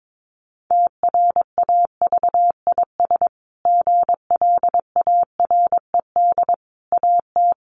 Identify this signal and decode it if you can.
Morse code, heard as a keyed tone, about 22 wpm, 700 Hz. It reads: TRAVIS GLARED AT
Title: A Sound Of Thunder 22wpm